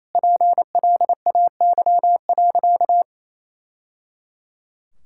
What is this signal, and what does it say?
Morse code, heard as a keyed tone, about 27 wpm, 700 Hz. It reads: PLAY.
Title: Sentences from Top 1000 Words - Review Part 5 - 28wpm